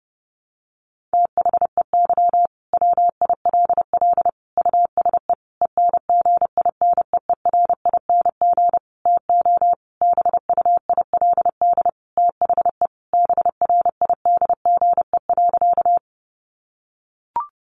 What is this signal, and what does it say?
Morse code, heard as a keyed tone, about 30 wpm, 700 Hz. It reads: THEY WILL USE ENGINEERING TO BUILD THE BRIDGE.
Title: Sentences from Top 1500 Words 30wpm